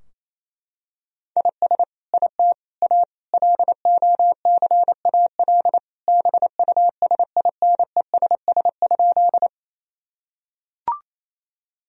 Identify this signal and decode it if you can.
Morse code, heard as a keyed tone, about 28 wpm, 700 Hz. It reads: IS IT A LOCAL BUSINESS?
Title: Sentences from Top 1000 Words - Review Part 3 - 28wpm